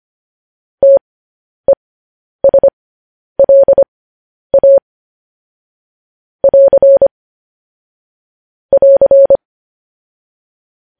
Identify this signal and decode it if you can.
Morse code, heard as a keyed tone, about 25 wpm, spaced as for 10 wpm, 550 Hz. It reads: TESLA <AR> <AR>
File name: podcast / News headlines-1x-25x10